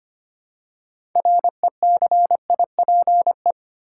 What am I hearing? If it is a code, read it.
Morse code, heard as a keyed tone, about 25 wpm, 700 Hz. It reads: RECIPE
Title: Six Letter Words - 25wpm